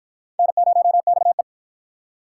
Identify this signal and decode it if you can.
Morse code, heard as a keyed tone, about 53 wpm, 700 Hz. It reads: N0XE